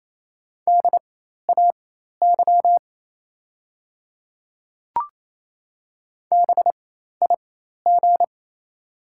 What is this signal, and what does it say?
Morse code, heard as a keyed tone, about 28 wpm, spaced as for 13 wpm, 690 Hz. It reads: DAY BIG
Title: 4x Character Spacing - Top 200 Words - 28wpm